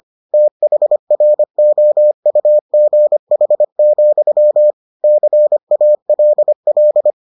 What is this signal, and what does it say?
Morse code, heard as a keyed tone, about 25 wpm, 600 Hz. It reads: THROUGH, CALL